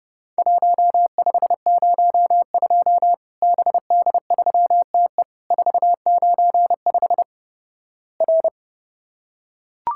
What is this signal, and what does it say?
Morse code, heard as a keyed tone, about 30 wpm, 700 Hz. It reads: 1502 BD3TE 495 R
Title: Worked All Europe DX Contest - QTC Exchange - 30wpm